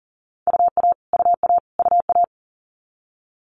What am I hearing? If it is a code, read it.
Morse code, heard as a keyed tone, about 42 wpm, 700 Hz. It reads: UA UA UA
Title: Multi-letter Phonemes Repeated 3x 40wpm